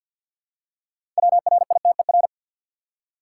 Morse code, heard as a keyed tone, about 52 wpm, 700 Hz. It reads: WRITER
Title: Six Letter Words - 50wpm